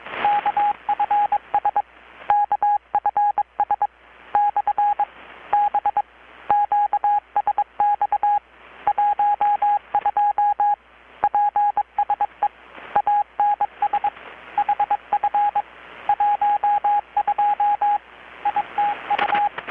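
Morse code, heard as a keyed tone, about 22 wpm, 820 Hz, with radio noise. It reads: KFS KFS / B QSX 12 PSE ANS HF 12 ITU
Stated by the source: Morse from KPH's Night of Nights XX maritime radiotelegraphy event, received in Chiapas, Mexico about 3,800 km away on a Kenwood R-600 with a grounded 10 m long-wire antenna.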